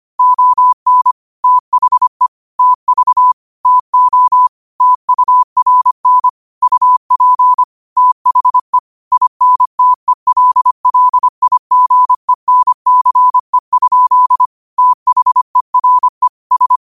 Morse code, heard as a keyed tone, about 25 wpm, 1000 Hz. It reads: ON THE TV TO TURN UP THE INTELLIGENCE? THERE S